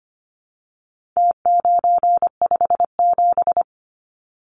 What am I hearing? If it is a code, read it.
Morse code, heard as a keyed tone, about 25 wpm, 700 Hz. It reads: T957
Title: Numbers - Contest Style - 4-Digit 25wpm